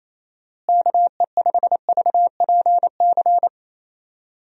Morse code, heard as a keyed tone, about 27 wpm, 700 Hz. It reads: KE5VPC
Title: USA Call Signs - 6 Characters - Repeated 3x - 28wpm